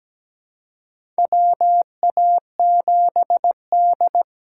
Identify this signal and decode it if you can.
Morse code, heard as a keyed tone, about 17 wpm, 700 Hz. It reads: WA7D
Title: USA Call Signs - 4 Characters - 6x Character Spacing - 17wpm